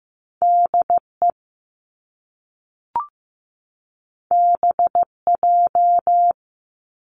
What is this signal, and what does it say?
Morse code, heard as a keyed tone, about 15 wpm, 700 Hz. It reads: DE BJ
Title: Top Two Letter Combinations in Words 15wpm